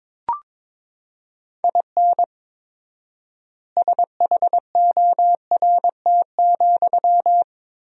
Morse code, heard as a keyed tone, about 22 wpm, 700 Hz. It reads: IN SHORT,
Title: 4x Word-Spacing - Sentences from Top 400 Words - 22wpm